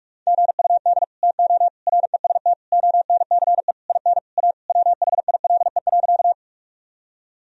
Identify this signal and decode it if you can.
Morse code, heard as a keyed tone, about 46 wpm, 700 Hz. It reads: GUD TO REST ONCE IN A WHILE.
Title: Sentences from Top 500 Words and Abbreviations 45wpm